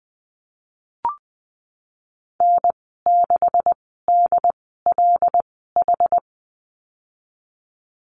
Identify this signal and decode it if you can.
Morse code, heard as a keyed tone, about 20 wpm, spaced as for 15 wpm, 700 Hz. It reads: N6DLH
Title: 193 Lesson 31g - Letter v - Call Signs 20wpm